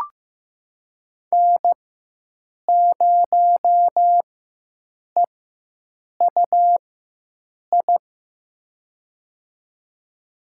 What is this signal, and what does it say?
Morse code, heard as a keyed tone, about 15 wpm, spaced as for 7 wpm, 700 Hz. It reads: N0EUI
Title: USA Call Signs - 5 Characters - 4x Character Spacing - 15wpm